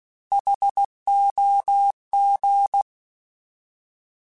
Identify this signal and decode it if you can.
Morse code, heard as a keyed tone, about 16 wpm, 800 Hz. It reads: HOG
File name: Alphabetical / Alphabetical mp3s / Hs @15